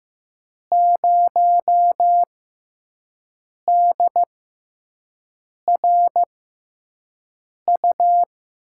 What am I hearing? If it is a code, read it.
Morse code, heard as a keyed tone, about 15 wpm, spaced as for 5 wpm, 700 Hz. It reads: 0DRU
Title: USA Call Signs - 5 Characters - 6x Character Spacing - 15wpm